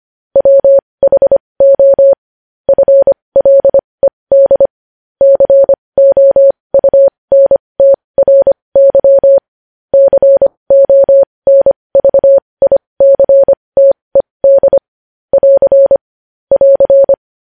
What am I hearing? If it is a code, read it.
Morse code, heard as a keyed tone, about 25 wpm, spaced as for 20 wpm, 550 Hz. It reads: WHO FLED COUNTRY CONVICTED <AR> <AR>
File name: podcast / News headlines-1x-25x20